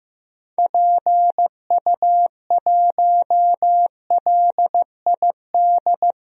Morse code, heard as a keyed tone, about 15 wpm, 700 Hz. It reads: PU1LID
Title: DXCC-100 Call Signs - 6 Characters - Repeated 3x - 15wpm